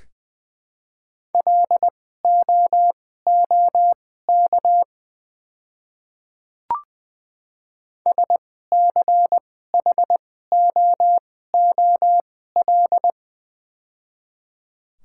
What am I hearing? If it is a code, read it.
Morse code, heard as a keyed tone, about 21 wpm, spaced as for 15 wpm, 700 Hz. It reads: LOOK SCHOOL
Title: 2x Character Spacing - Top 300 Words - 20wpm